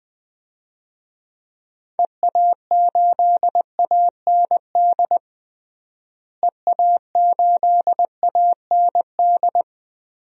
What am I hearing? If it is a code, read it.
Morse code, heard as a keyed tone, about 21 wpm, 700 Hz. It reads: EA8AND EA8AND
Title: DXCC-100 Call Signs - 6 Characters - Repeated 2x - 20wpm